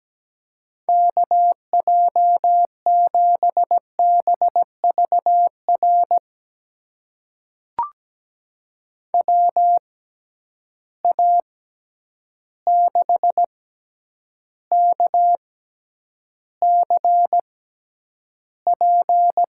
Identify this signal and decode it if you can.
Morse code, heard as a keyed tone, about 17 wpm, spaced as for 9 wpm, 700 Hz. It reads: KJ7BVR WA6KCP
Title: USA Call Signs - 6 Characters - 6x Character Spacing - 17wpm